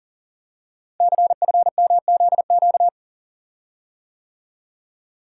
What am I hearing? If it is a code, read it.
Morse code, heard as a keyed tone, about 40 wpm, 690 Hz. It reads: CFMZQ
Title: Code Groups - 5 Letters - 40wpm